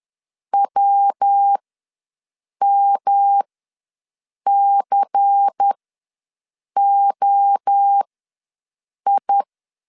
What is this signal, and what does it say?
Morse code, heard as a keyed tone, about 10 wpm, spaced as for 6 wpm, 800 Hz. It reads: WMCOI
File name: LEZIONI / 5 LBVC / 5 LBVC ws 10wpm 3733